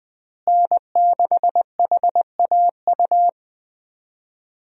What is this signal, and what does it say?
Morse code, heard as a keyed tone, about 21 wpm, 700 Hz. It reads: N6HAU
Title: USA Call Signs - 5 Characters - Repeated 2x - 20wpm